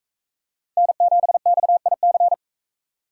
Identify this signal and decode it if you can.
Morse code, heard as a keyed tone, about 41 wpm, 700 Hz. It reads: N7XIC